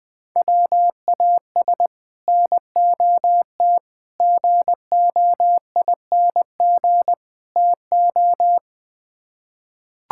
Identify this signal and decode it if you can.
Morse code, heard as a keyed tone, about 21 wpm, 700 Hz. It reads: WAS NOT GOING TO
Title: Build A Phrase - 4-Word Phrases - 20wpm